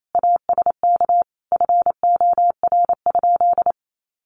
Morse code, heard as a keyed tone, about 28 wpm, 700 Hz. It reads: ASK FOR?
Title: Sentences from Top 200 Words 28wpm